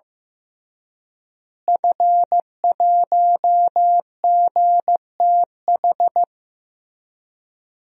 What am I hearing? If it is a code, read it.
Morse code, heard as a keyed tone, about 15 wpm, 700 Hz. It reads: F1GTH
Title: DXCC-100 Call Signs - 5 Characters - Repeated 3x - 15wpm